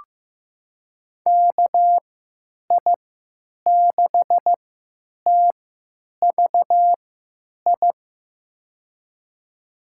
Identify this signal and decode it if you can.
Morse code, heard as a keyed tone, about 15 wpm, spaced as for 9 wpm, 700 Hz. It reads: KI6TVI
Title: USA Call Signs - 6 Characters - 3x Character Spacing - 15wpm